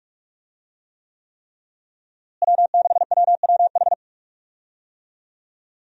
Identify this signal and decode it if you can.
Morse code, heard as a keyed tone, about 46 wpm, 700 Hz. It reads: W6WWH
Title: USA Call Signs - 5 Characters - Repeated 3x - 45wpm